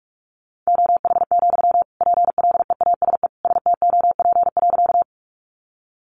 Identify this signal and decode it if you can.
Morse code, heard as a keyed tone, about 44 wpm, 700 Hz. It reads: OH, PLEASE STOP.
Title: Sentences from Top 1500 Words 45wpm